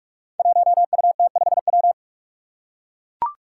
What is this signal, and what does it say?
Morse code, heard as a keyed tone, about 41 wpm, 700 Hz. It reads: 1UT5W
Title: Five Challenging Letter-Number Combinations 45wpm